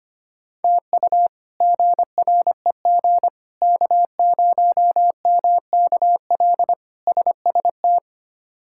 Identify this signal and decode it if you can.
Morse code, heard as a keyed tone, about 25 wpm, 700 Hz. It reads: TU GREG K0MKL SST
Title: K1USN SST QSO Practice - 25wpm